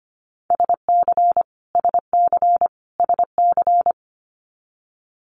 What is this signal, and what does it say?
Morse code, heard as a keyed tone, about 25 wpm, 700 Hz. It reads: SC SC SC
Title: US States Repeated 3x 25wpm